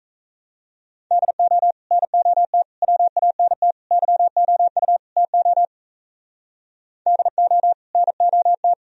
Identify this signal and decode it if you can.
Morse code, heard as a keyed tone, about 35 wpm, 700 Hz. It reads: DO NOT WANT YOU TO DO NOT
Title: Speed-Racing - Sets of 5 Words 28wpm